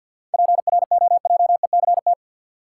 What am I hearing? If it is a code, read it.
Morse code, heard as a keyed tone, about 53 wpm, 700 Hz. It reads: PROJECT
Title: Sentences from Top 1400 Words - New Word Drill Alt - 50wpm